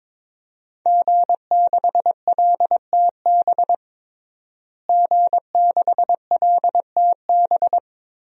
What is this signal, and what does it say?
Morse code, heard as a keyed tone, about 22 wpm, 700 Hz. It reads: G6LTB G6LTB
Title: DXCC-100 Call Signs - 5 Characters - Repeated 2x - 22wpm